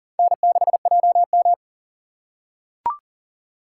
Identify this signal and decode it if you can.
Morse code, heard as a keyed tone, about 40 wpm, 700 Hz. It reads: N6JM
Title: USA Call Signs - 4 Characters - 2x Character Spacing - 40wpm